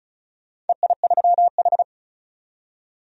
Morse code, heard as a keyed tone, about 36 wpm, 700 Hz. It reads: EI3H
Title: DXCC-100 Call Signs - 4 Characters - 35wpm